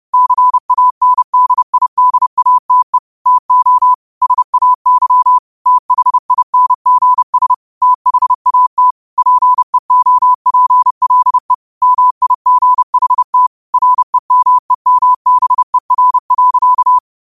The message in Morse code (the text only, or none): CANDIDATE TO SAY THINGS THAT PEOPLE MIGHT REMEMBER.